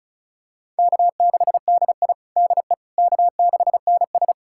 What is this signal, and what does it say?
Morse code, heard as a keyed tone, about 36 wpm, 700 Hz. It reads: K6DI DE K6NS
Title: USA Callsign DE Callsign - 4 Characters - 35wpm